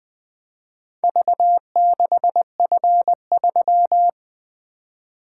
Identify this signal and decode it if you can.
Morse code, heard as a keyed tone, about 21 wpm, 700 Hz. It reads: V6F3
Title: Four Letter-Number Combinations - 20wpm